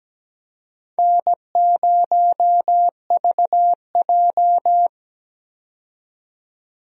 Morse code, heard as a keyed tone, about 17 wpm, 700 Hz. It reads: N0VJ
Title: USA Call Signs - 4 Characters - 17wpm